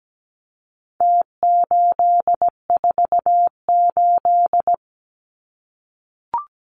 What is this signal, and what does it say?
Morse code, heard as a keyed tone, about 17 wpm, 700 Hz. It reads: T848
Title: Numbers - Contest Style - 4-Digit 17wpm